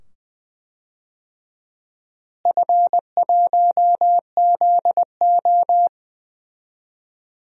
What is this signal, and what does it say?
Morse code, heard as a keyed tone, about 20 wpm, 700 Hz. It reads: F1ZO
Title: DXCC-100 Call Signs - 4 Characters - Repeated 2x - 20wpm